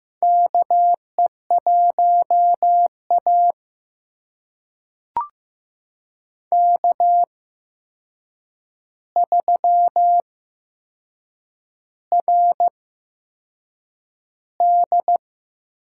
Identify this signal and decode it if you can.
Morse code, heard as a keyed tone, about 15 wpm, spaced as for 6 wpm, 700 Hz. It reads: KE1A K3RD
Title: USA Call Signs - 4 Characters - 8x Character Spacing - 15wpm